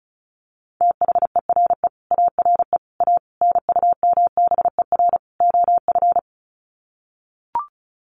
Morse code, heard as a keyed tone, about 34 wpm, 700 Hz. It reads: THERE ARE A NUMBER OF
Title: Sets of 5 Words 35wpm